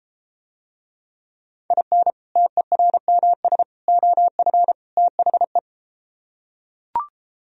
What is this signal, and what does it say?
Morse code, heard as a keyed tone, about 32 wpm, 700 Hz. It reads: IN TERMS OF THE